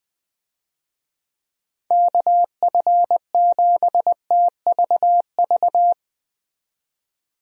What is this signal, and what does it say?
Morse code, heard as a keyed tone, about 20 wpm, 700 Hz. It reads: KF7TVV
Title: USA Call Signs - 6 Characters - 20wpm